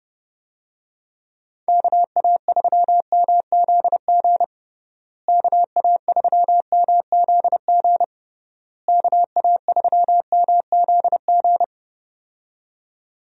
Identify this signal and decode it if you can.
Morse code, heard as a keyed tone, about 30 wpm, 700 Hz. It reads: KA3MZG KA3MZG KA3MZG
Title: USA Call Signs - 6 Characters - Repeated 3x - 30wpm